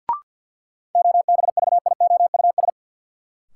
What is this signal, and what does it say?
Morse code, heard as a keyed tone, about 53 wpm, 700 Hz. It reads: OBVIOUS
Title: Sentences from Top 1600 Words - New Word Drill - 50wpm